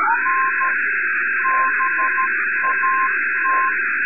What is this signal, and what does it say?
Morse code, heard as a keyed tone, about 13 wpm, 1050 Hz, with noise. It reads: N BAT
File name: sat / RADIOSCAF-B 24Dec2011